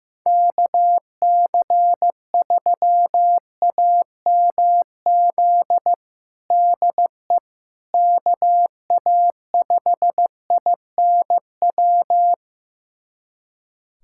Morse code, heard as a keyed tone, about 15 wpm, 700 Hz. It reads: KC3AMZ DE KA5INW
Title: USA Callsign DE Callsign - 6 Characters - 15wpm